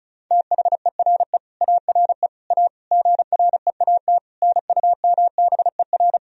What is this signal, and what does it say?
Morse code, heard as a keyed tone, about 36 wpm, 700 Hz. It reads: THERE ARE A GREAT NUMBER
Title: Sentences from Top 1000 Words - Review Part 1 - 35wpm